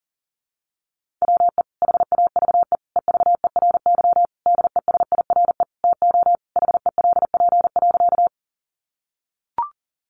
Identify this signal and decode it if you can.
Morse code, heard as a keyed tone, about 40 wpm, 700 Hz. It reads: WE HAVE EVERY DESIRE TO HELP.
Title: Sentences from Top 1100 Words 40wpm